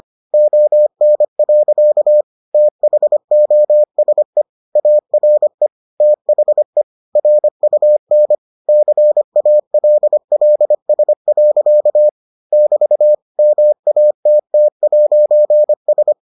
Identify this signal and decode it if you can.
Morse code, heard as a keyed tone, about 25 wpm, 600 Hz. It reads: ON. THOSE ARE THE RUN CALLS. <BT> MATT'S